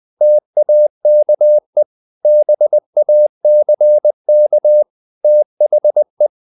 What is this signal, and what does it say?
Morse code, heard as a keyed tone, about 20 wpm, 600 Hz. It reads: TAKE BACK THE
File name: the old man and the sea / the old man and the sea part 1 20wpm